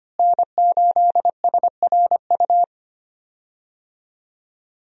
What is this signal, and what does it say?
Morse code, heard as a keyed tone, about 25 wpm, 700 Hz. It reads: N8SRU
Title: USA Call Signs - 5 Characters - Repeated 2x - 25wpm